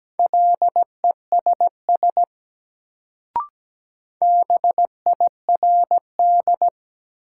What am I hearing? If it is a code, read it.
Morse code, heard as a keyed tone, about 17 wpm, 700 Hz. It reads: LESS BIRD
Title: Sentences from Top 400 Words - New Word Drill - 17wpm